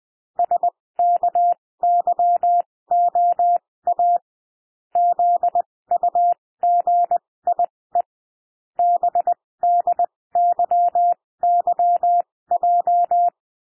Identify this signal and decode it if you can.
Morse code, heard as a keyed tone, about 20 wpm, spaced as for 16 wpm, 700 Hz. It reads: SKYOA ZUGIE BDYYJ
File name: cwmp3 / L1 SOUND LETTERS 20 wpm